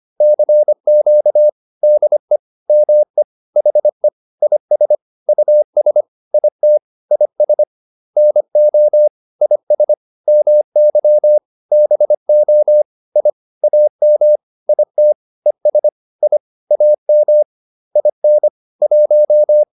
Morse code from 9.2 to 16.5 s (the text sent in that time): IS MY BO I AM IT ES I